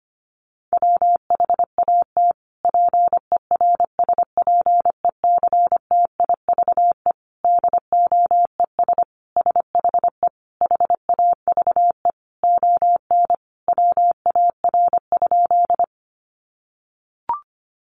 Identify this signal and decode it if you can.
Morse code, heard as a keyed tone, about 25 wpm, 700 Hz. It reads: WHAT PERSPECTIVE DOES SHE HAVE ON WAR?
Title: Sentences from Top 1200 Words 25wpm